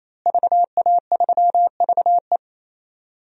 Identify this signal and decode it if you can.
Morse code, heard as a keyed tone, about 28 wpm, 700 Hz. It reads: VA3VE
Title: DXCC-100 Call Signs - 5 Characters - Repeated 3x - 28wpm